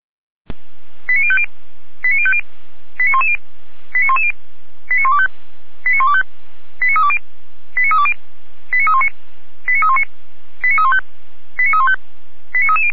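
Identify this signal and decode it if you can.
Five-tone selective call (ZVEI), heard as quick runs of five tones, tones of about 70 ms. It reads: 89050 89050 89110 89110 89125 89125 89320 89320 89319 89319 89315 89315 89330
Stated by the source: ZVEI five-tone selective call 89042, received by SDR.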